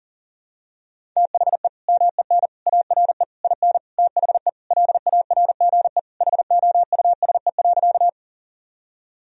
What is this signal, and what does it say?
Morse code, heard as a keyed tone, about 37 wpm, 700 Hz. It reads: THE MEN ARE IN THE LARGE HOUSE.
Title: Sentences from Top 1000 Words - Review Part 1 - 40wpm